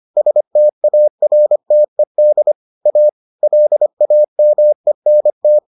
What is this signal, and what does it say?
Morse code, heard as a keyed tone, about 25 wpm, 600 Hz. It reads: STARTED A LAMENT